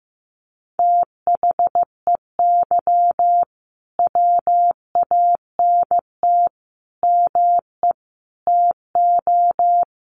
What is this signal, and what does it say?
Morse code, heard as a keyed tone, about 15 wpm, 700 Hz. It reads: THEY WANT ME TO
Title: Sentences from Top 900 Words 15wpm